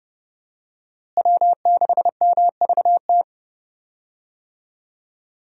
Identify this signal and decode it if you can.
Morse code, heard as a keyed tone, about 30 wpm, 700 Hz. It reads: W6MVT
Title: USA Call Signs - 5 Characters - Repeated 2x - 30wpm